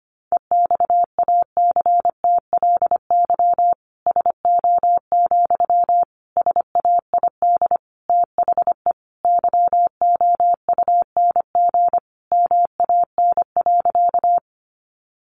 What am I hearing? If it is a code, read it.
Morse code, heard as a keyed tone, about 25 wpm, 700 Hz. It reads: EXACTLY SO, SAID THE YOUNG MAN.
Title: Sentences from Top 900 Words 25wpm